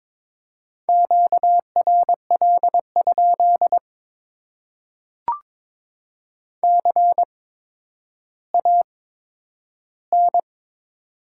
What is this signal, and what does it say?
Morse code, heard as a keyed tone, about 21 wpm, spaced as for 11 wpm, 700 Hz. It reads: QRL? CAN
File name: Top 90 QSO Elements - 8x Character Spacing - 22wpm